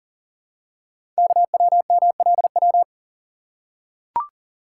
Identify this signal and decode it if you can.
Morse code, heard as a keyed tone, about 39 wpm, 700 Hz. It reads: KWMLW
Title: Code Groups - 5 Letters - 40wpm